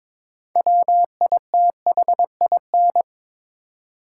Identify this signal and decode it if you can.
Morse code, heard as a keyed tone, about 22 wpm, 700 Hz. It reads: WITHIN
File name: Build A Phrase - 2-Word Phrases - 22wpm